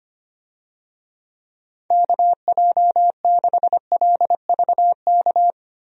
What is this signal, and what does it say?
Morse code, heard as a keyed tone, about 25 wpm, 700 Hz. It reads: KJ6LVK
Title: USA Call Signs - 6 Characters - Repeated 3x - 25wpm